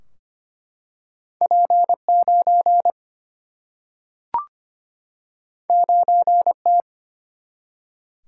Morse code, heard as a keyed tone, about 25 wpm, 700 Hz. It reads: P 9 9 T
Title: Single-Letter Single-Number 25wpm